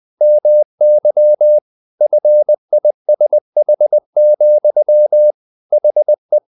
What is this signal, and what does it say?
Morse code, heard as a keyed tone, about 20 wpm, 600 Hz. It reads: MY FISH, HE
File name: the old man and the sea part 2 20wpm